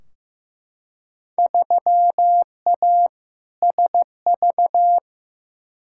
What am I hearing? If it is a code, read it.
Morse code, heard as a keyed tone, about 15 wpm, 700 Hz. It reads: 3A SV
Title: Field Day Exchanges - 15wpm